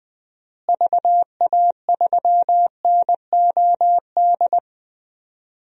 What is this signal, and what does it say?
Morse code, heard as a keyed tone, about 20 wpm, 700 Hz. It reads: VA3NOD
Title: DXCC-100 Call Signs - 6 Characters - Repeated 2x - 20wpm